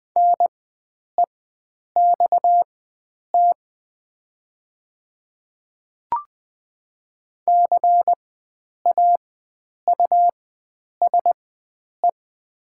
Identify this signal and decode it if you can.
Morse code, heard as a keyed tone, about 21 wpm, spaced as for 9 wpm, 690 Hz. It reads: NEXT CAUSE
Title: 4x Character Spacing - Top 500 Words - 20wpm